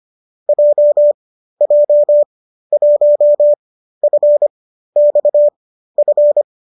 Morse code, heard as a keyed tone, about 25 wpm, spaced as for 13 wpm, 590 Hz. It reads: JJ1FXF